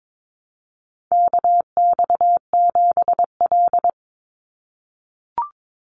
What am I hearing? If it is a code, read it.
Morse code, heard as a keyed tone, about 22 wpm, 700 Hz. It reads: KX7L
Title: NA SOTA Chaser Callsigns 22wpm